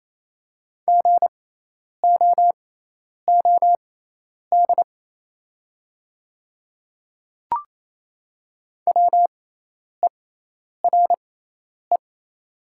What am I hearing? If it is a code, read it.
Morse code, heard as a keyed tone, about 28 wpm, spaced as for 10 wpm, 700 Hz. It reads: GOOD WERE
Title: 6x Character Spacing - Top 200 Words - 28wpm